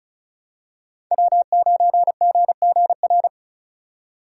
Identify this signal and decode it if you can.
Morse code, heard as a keyed tone, about 36 wpm, 690 Hz. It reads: W9GGR
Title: USA Call Signs - 5 Characters - 3x Character Spacing - 35wpm